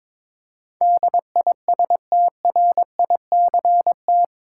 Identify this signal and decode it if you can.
Morse code, heard as a keyed tone, about 22 wpm, 700 Hz. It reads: DISTRICT
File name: practice / 22wpm / Eight Letter Words - 22wpm